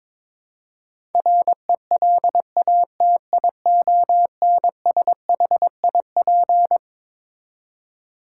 Morse code, heard as a keyed tone, about 22 wpm, 700 Hz. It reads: RELATIONSHIP
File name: Twelve Letter Words - 22wpm